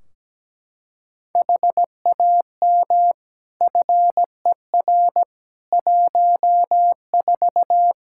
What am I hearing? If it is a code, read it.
Morse code, heard as a keyed tone, about 17 wpm, 700 Hz. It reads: HAM FER 14
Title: K1USN SST QSO Practice 17wpm